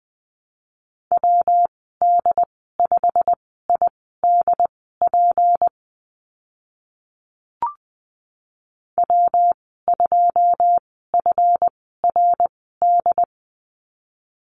Morse code, heard as a keyed tone, about 20 wpm, spaced as for 14 wpm, 700 Hz. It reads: WD5IDP W2FRD
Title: 167 Lesson 27g - Number 7 - Call Signs 20wpm